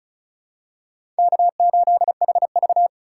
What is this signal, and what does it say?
Morse code, heard as a keyed tone, about 35 wpm, 700 Hz. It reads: K8HV
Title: USA Call Signs - 4 Characters - 3x Character Spacing - 35wpm